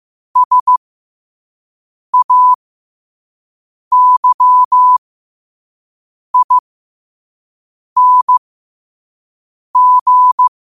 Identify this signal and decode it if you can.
Morse code, heard as a keyed tone, about 15 wpm, spaced as for 5 wpm, 1000 Hz. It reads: SAYING
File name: qotdfiles / 5wpm